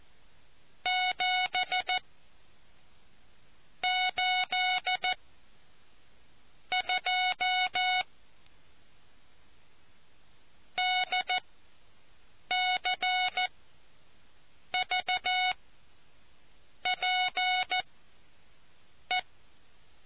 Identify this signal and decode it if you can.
Morse code, heard as a keyed tone, about 14 wpm, spaced as for 5 wpm, 740 Hz. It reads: 7 82 DCVPE